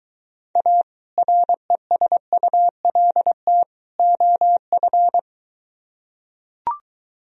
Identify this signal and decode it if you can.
Morse code, heard as a keyed tone, about 23 wpm, 700 Hz. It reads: A RESULT OF